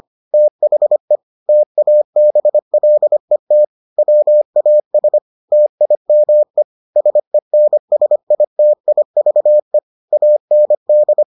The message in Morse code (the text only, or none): THE TABLET WAS TIME SENSITIVE AND